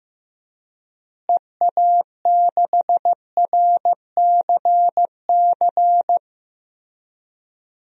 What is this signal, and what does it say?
Morse code, heard as a keyed tone, about 15 wpm, 700 Hz. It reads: EA6RCC
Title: DXCC-100 Call Signs - 6 Characters - Repeated 3x - 15wpm